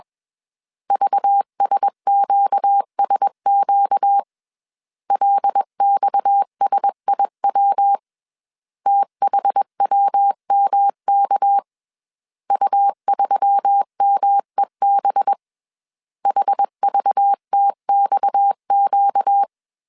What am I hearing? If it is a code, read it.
Morse code, heard as a keyed tone, about 21 wpm, 800 Hz. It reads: VSQSQ LXSIW THWMK U3MEB HVTXQ
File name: CW / DET / 1 6 1500C S B 20wpm 2048